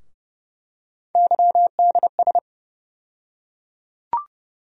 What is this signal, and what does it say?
Morse code, heard as a keyed tone, about 30 wpm, 700 Hz. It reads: YDS
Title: USA Call Sign Postfixes - 3 Letters 30wpm